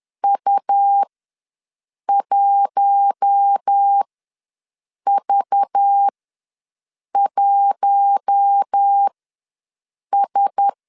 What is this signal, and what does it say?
Morse code, heard as a keyed tone, about 10 wpm, spaced as for 6 wpm, 800 Hz. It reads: U1V1S
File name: LEZIONI / 5 LBVC / 5 LBV ws 10wpm 3645